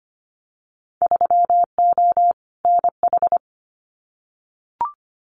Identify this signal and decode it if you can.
Morse code, heard as a keyed tone, about 25 wpm, 700 Hz. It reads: 3O NH
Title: Winter Field Day Exchanges Encore 25wpm